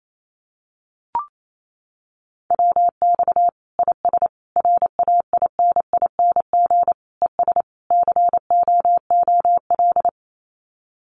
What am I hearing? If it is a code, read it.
Morse code, heard as a keyed tone, about 28 wpm, 700 Hz. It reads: WX IS RAINING ES COOL
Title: Common QSO Phrases 28wpm